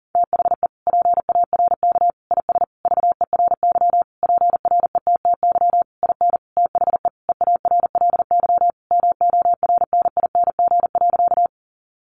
Morse code, heard as a keyed tone, about 40 wpm, 700 Hz. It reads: THE PARK IS VERY PRETTY IN THE EARLY MORNING.
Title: Sentences from Top 1300 Words 40wpm